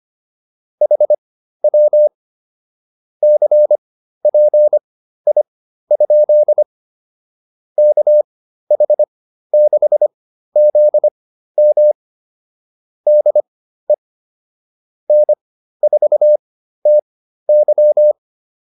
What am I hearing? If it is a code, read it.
Morse code, heard as a keyed tone, about 25 wpm, spaced as for 13 wpm, 600 Hz. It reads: HW CPI? KH6ZM DE N4TY
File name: QSO109 13